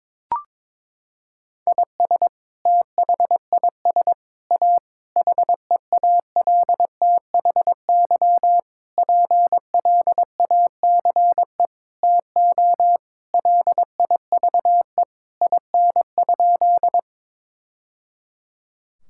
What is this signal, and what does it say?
Morse code, heard as a keyed tone, about 22 wpm, 700 Hz. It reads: IS THIS A HEALTHY PLACE TO LIVE IN?
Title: Sentences from Top 1000 Words - Review Part 5 - 22wpm